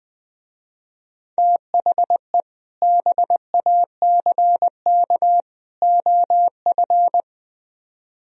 Morse code, heard as a keyed tone, about 21 wpm, 700 Hz. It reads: THE BACK OF
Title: Speed-Racing - Sets of 3 Words 20wpm